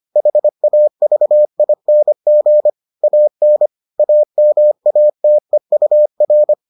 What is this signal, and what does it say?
Morse code, heard as a keyed tone, about 25 wpm, 600 Hz. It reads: HAVING AN AMATEUR